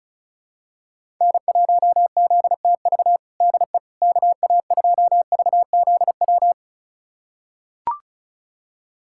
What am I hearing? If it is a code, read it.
Morse code, heard as a keyed tone, about 34 wpm, 700 Hz. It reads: N1ZTV DE KA2VZW
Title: USA Callsign DE Callsign 35wpm